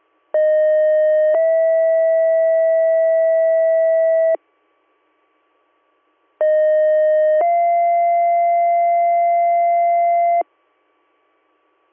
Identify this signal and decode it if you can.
Two-tone page, heard as a one-second tone, then a three-second tone, about 620 then 650 Hz, about 620 then 690 Hz. It is heard from 0.3 to 10.4 s.